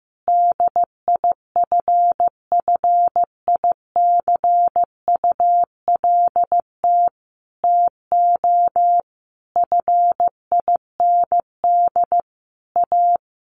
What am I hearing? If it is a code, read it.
Morse code, heard as a keyed tone, about 15 wpm, 700 Hz. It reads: DIFFICULT TO FIND A
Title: Sentences from Top 1100 Words 15wpm